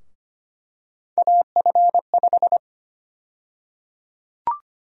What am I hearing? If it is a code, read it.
Morse code, heard as a keyed tone, about 25 wpm, 700 Hz. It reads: AF5
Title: USA Call Sign Prefixes - Repeated 2x - 25wpm